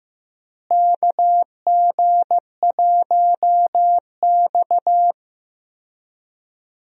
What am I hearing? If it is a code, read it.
Morse code, heard as a keyed tone, about 16 wpm, 700 Hz. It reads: KG1X
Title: USA Call Signs - 4 Characters - Repeated 2x - 15wpm